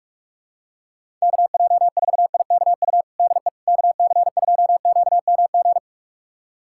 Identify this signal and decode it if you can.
Morse code, heard as a keyed tone, about 46 wpm, 700 Hz. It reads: KJ4IKU DE KC2QMG
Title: USA Callsign DE Callsign - 6 Characters - 45wpm